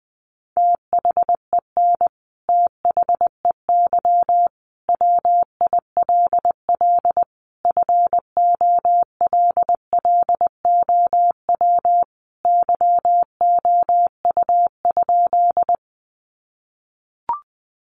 Morse code, heard as a keyed tone, about 20 wpm, 700 Hz. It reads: THEN THEY WILL FOLLOW YOU?
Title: Sentences from Top 200 Words 20wpm